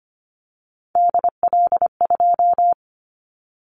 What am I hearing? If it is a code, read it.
Morse code, heard as a keyed tone, about 25 wpm, 700 Hz. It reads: DL2
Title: Callsign Prefixes for Top 25 DXCC - 25wpm